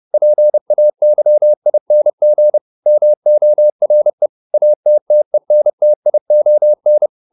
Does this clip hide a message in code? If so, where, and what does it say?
Yes: Morse code from 0.1 to 7.1 s, heard as a keyed tone, about 30 wpm, 600 Hz. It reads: PAYING MORE ATTENTION